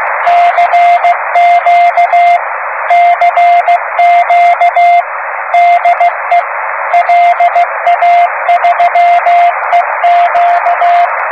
Morse code, heard as a keyed tone, about 15 wpm, 700 Hz, with radio noise. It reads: CQ CQ DE LA3EQ
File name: om3pv 70mhz / zaznam / la3eq